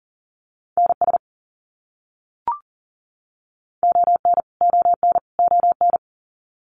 Morse code, heard as a keyed tone, about 40 wpm, 700 Hz. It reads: NS ON ON ON
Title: Canadian Provinces and Territories - Repeated 3x - 40wpm